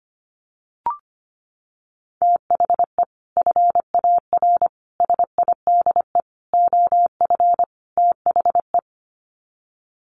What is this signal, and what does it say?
Morse code, heard as a keyed tone, about 25 wpm, 700 Hz. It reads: THE FAR SIDE OF THE
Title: Sets of 5 Words 25wpm